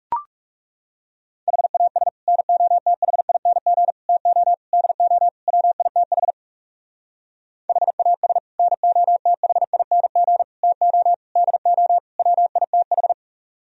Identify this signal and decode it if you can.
Morse code, heard as a keyed tone, about 36 wpm, 700 Hz. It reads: HAS NOTHING TO DO WITH HAS NOTHING TO DO WITH
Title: Speed-Racing - Sets of 5 Words 30wpm